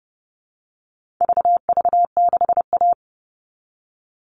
Morse code, heard as a keyed tone, about 30 wpm, 700 Hz. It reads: VV6A
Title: All Cut Numbers - 4-Digit 30wpm